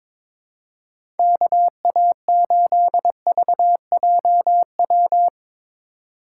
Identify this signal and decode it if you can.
Morse code, heard as a keyed tone, about 23 wpm, 700 Hz. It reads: KA8VJW
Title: USA Call Signs - 6 Characters - 4x Character Spacing - 22wpm